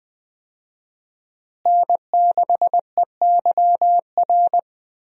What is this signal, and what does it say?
Morse code, heard as a keyed tone, about 20 wpm, 700 Hz. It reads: N6EYR
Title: USA Call Signs - 5 Characters - Repeated 2x - 20wpm